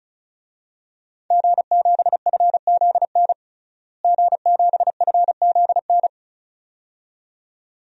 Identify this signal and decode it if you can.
Morse code, heard as a keyed tone, about 34 wpm, 700 Hz. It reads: G7FZN G7FZN
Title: DXCC-100 Call Signs - 5 Characters - Repeated 2x - 35wpm